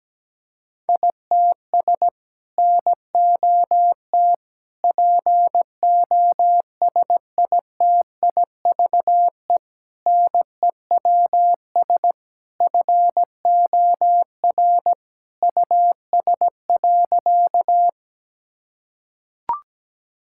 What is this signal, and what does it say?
Morse code, heard as a keyed tone, about 17 wpm, 700 Hz. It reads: ITS NOT POSITIVE NEWS FOR US.
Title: Sentences from Top 1000 Words - Review Part 5 - 17wpm